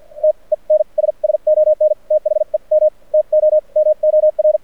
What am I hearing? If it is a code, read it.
Morse code, heard as a keyed tone, about 52 wpm, spaced as for 39 wpm, 610 Hz. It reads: TENS SON THEM TOMOR
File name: archived / CRSnet